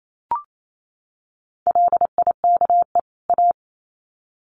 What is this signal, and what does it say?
Morse code, heard as a keyed tone, about 28 wpm, 700 Hz. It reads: LIKE A
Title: Sets of 2 Words 28wpm